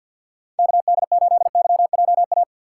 Morse code, heard as a keyed tone, about 52 wpm, 700 Hz. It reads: KD8YJA
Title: USA Call Signs - 2x Character Spacing - 50wpm